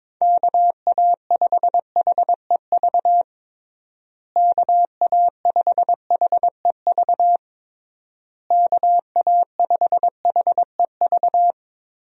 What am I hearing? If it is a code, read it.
Morse code, heard as a keyed tone, about 22 wpm, 700 Hz. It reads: KA5HEV KA5HEV KA5HEV
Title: USA Call Signs - 6 Characters - Repeated 3x - 22wpm